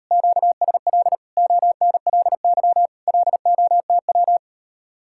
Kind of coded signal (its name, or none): Morse code